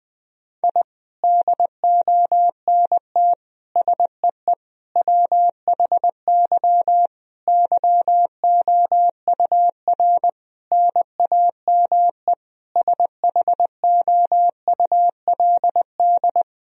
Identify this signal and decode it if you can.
Morse code, heard as a keyed tone, about 21 wpm, 700 Hz. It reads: I DONT SEE WHY YOUR NAME SHOULD
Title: Sentences from Top 1000 Words - Review Part 2 - 20wpm